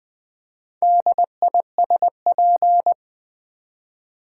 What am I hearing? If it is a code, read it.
Morse code, heard as a keyed tone, about 20 wpm, 700 Hz. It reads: DISP
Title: Build A Word - 2 Letters At A Time - 8-Letter Words - 20wpm